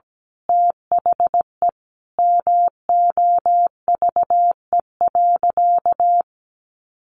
Morse code, heard as a keyed tone, about 17 wpm, 700 Hz. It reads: THE MOVE.
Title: Sentences from Top 1400 Words 17wpm